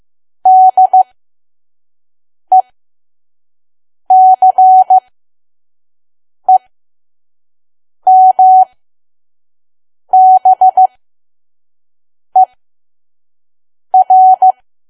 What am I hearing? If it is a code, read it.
Morse code, heard as a keyed tone, about 15 wpm, spaced as for 5 wpm, 750 Hz. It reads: DECEMBER
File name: Archive / 5 WPM files / PM